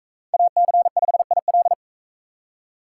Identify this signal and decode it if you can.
Morse code, heard as a keyed tone, about 42 wpm, 700 Hz. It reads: AC5IL